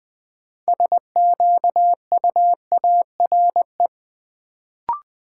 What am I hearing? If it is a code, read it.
Morse code, heard as a keyed tone, about 20 wpm, 700 Hz. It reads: SQUARE
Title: Sentences from Top 1200 Words - New Word Drill Alt - 20wpm